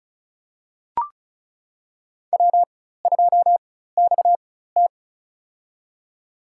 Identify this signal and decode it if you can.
Morse code, heard as a keyed tone, about 36 wpm, spaced as for 16 wpm, 700 Hz. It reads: W2XT
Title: USA Call Signs - 4 Characters - 4x Character Spacing - 35wpm